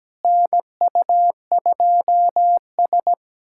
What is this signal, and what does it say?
Morse code, heard as a keyed tone, about 17 wpm, 700 Hz. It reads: NU2S
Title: USA Callsign DE Callsign - 4 Characters - 17wpm